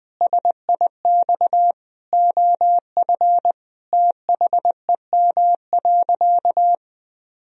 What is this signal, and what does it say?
Morse code, heard as a keyed tone, about 20 wpm, 700 Hz. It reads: SIX OF THEM.
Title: Sentences from Top 1000 Words - Review Part 2 - 20wpm